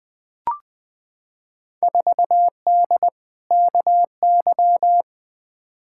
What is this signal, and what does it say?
Morse code, heard as a keyed tone, about 20 wpm, 700 Hz. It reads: 4D KY
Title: Field Day Exchanges Encore - 20wpm